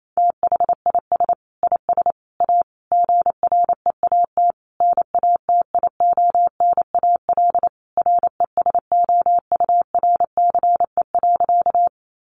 Morse code, heard as a keyed tone, about 28 wpm, 700 Hz. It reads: THIS IS A GREAT NATIONAL RESOURCE.
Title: Sentences from Top 1300 Words 28wpm